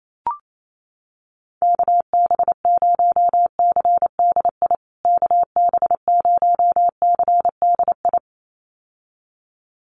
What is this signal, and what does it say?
Morse code, heard as a keyed tone, about 28 wpm, 700 Hz. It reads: KB0CDI KB0CDI
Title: USA Call Signs Repeated 2x 28wpm